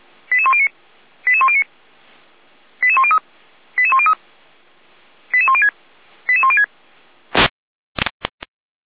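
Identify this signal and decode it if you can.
Five-tone selective call (ZVEI), heard as quick runs of five tones, tones of about 70 ms. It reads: 88109 88109 88103 88103 88106 88106
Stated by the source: ZVEI five-tone call 88102 received on an SDR.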